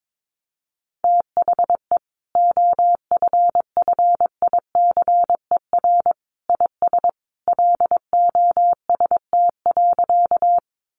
Morse code, heard as a keyed tone, about 22 wpm, 700 Hz. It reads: THE OFFICER IS LOST.
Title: Sentences from Top 1400 Words 22wpm